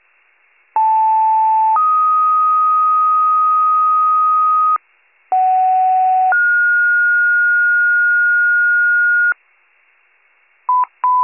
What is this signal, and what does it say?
Two-tone page, heard as a one-second tone, then a three-second tone, about 880 then 1250 Hz, about 750 then 1430 Hz. It is heard from 0.8 to 9.3 s.